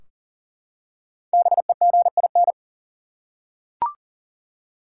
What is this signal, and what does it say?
Morse code, heard as a keyed tone, about 38 wpm, 700 Hz. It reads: BEGIN
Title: Five Letter Words - 40wpm